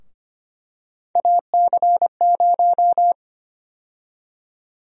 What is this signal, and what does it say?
Morse code, heard as a keyed tone, about 25 wpm, 700 Hz. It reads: AC0
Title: USA Call Sign Prefixes - Repeated 2x - 25wpm